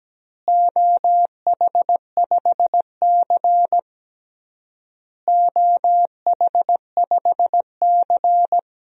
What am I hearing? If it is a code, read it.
Morse code, heard as a keyed tone, about 17 wpm, 700 Hz. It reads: OH5C OH5C
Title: DXCC-100 Call Signs - 4 Characters - Repeated 3x - 17wpm